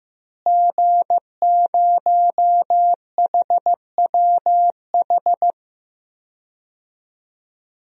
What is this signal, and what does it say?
Morse code, heard as a keyed tone, about 15 wpm, 700 Hz. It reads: G0HWH
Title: DXCC-100 Call Signs - 5 Characters - Repeated 3x - 15wpm